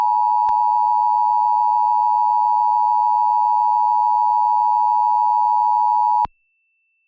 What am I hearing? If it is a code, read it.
Emergency Alert System (SAME) tones (the 853 and 960 Hz attention tone). It is heard from 0.6 to 6.2 s.
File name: wrct-log hi- 12